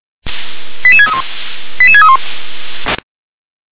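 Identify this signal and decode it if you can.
Five-tone selective call (ZVEI), heard as quick runs of five tones, tones of about 70 ms. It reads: 88631 88631
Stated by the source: ZVEI five-tone selective call received by SDR.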